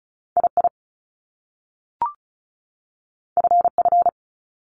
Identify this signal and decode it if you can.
Morse code, heard as a keyed tone, about 36 wpm, 700 Hz. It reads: II FF
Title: Top Two Letter Combinations in Words 35wpm